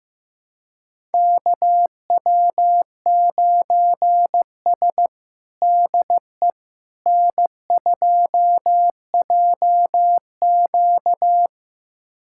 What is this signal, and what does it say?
Morse code, heard as a keyed tone, about 16 wpm, 700 Hz. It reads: KW9S DE N2JQ
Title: USA Callsign DE Callsign - 4 Characters - 15wpm